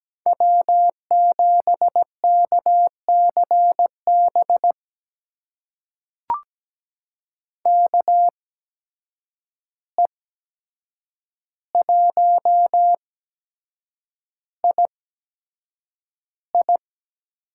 Morse code, heard as a keyed tone, about 17 wpm, spaced as for 7 wpm, 700 Hz. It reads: W7KCB KE1II
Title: USA Call Signs - 5 Characters - 8x Character Spacing - 17wpm